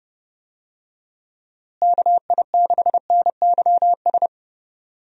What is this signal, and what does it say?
Morse code, heard as a keyed tone, about 32 wpm, 700 Hz. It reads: KI6NYS
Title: USA Call Signs - 6 Characters - Repeated 3x - 30wpm